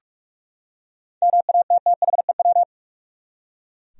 Morse code, heard as a keyed tone, about 46 wpm, 700 Hz. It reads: MATTHEW
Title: USA Common Names - 45wpm